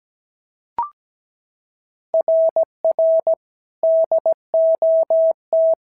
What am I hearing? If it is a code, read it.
Morse code, heard as a keyed tone, about 16 wpm, 650 Hz. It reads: RR DOT
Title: SOTA QSO Practice 17wpm